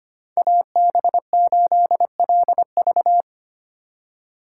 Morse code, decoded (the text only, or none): AB8LV